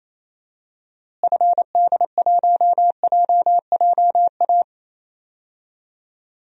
Morse code, heard as a keyed tone, about 28 wpm, 700 Hz. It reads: FD1JJA
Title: DXCC-100 Call Signs - 6 Characters - Repeated 3x - 28wpm